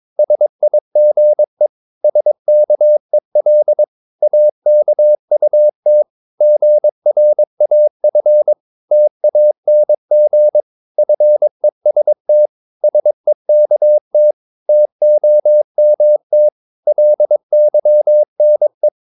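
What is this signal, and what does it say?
Morse code, heard as a keyed tone, about 22 wpm, 600 Hz. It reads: SIGE SKEL AKUT GRAF TANG FEST SEKT TOMT LYNE
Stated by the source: Morse practice of short Danish words at 22 wpm, from CW med Gnister.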